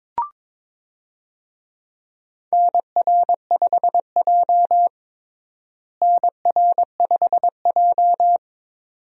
Morse code, heard as a keyed tone, about 22 wpm, 700 Hz. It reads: NR5J NR5J
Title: USA Call Signs - 4 Characters - Repeated 2x - 22wpm